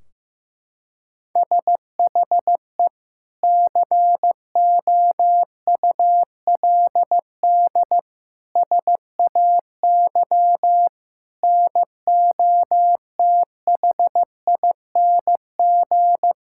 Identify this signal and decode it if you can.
Morse code, heard as a keyed tone, about 15 wpm, 700 Hz. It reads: SHE COULD SAY NOTHING
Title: Sentences from Top 1600 Words 15wpm